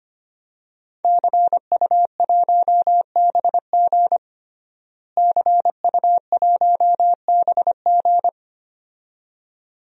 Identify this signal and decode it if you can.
Morse code, heard as a keyed tone, about 25 wpm, 700 Hz. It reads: CU1BG CU1BG
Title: DXCC-100 Call Signs - 5 Characters - Repeated 2x - 25wpm